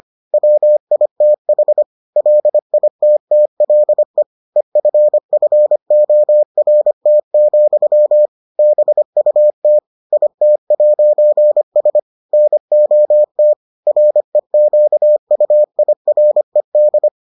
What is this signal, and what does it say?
Morse code, heard as a keyed tone, about 25 wpm, 600 Hz. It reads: WITH LITTLE EFFORT, BUT IT'S NOT REQUIRED